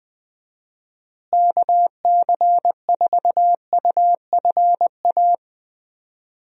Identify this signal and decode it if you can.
Morse code, heard as a keyed tone, about 20 wpm, 700 Hz. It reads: KC4UFA
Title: USA Call Signs - 6 Characters - Repeated 2x - 20wpm